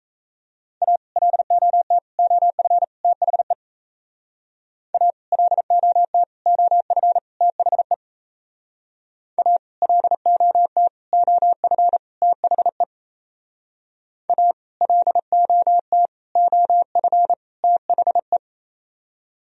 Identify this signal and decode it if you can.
Morse code, heard as a keyed tone, about 34 wpm, 700 Hz. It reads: A LOT OF THE A LOT OF THE A LOT OF THE A LOT OF THE
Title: Speed-Racing - Sets of 4 Words 28wpm